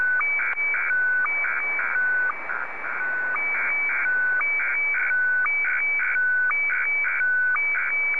SSTV (slow-scan television), Scottie DX.
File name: sstv / SDX-VK2GHO